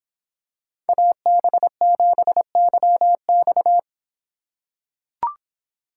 Morse code, heard as a keyed tone, about 26 wpm, 700 Hz. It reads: AB7YX